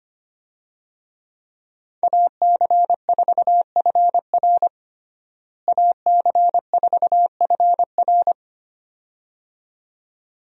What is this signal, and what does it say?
Morse code, heard as a keyed tone, about 25 wpm, 700 Hz. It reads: AC4FR AC4FR
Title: USA Call Signs - 5 Characters - Repeated 2x - 25wpm